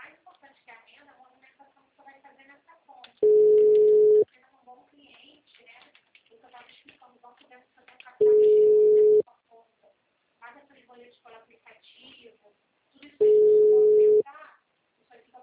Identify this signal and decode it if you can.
Ringback tone (Europe, 425 Hz), with speech.
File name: out- -2018- - - .46686